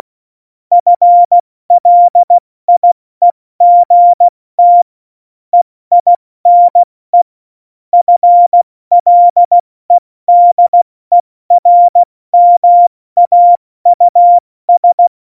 Morse code, heard as a keyed tone, about 16 wpm, 700 Hz. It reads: FLIEGT EINE FLEDERMAUS
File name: Morsekurs / Witze / Witz 26. Juni 2018, 70 F80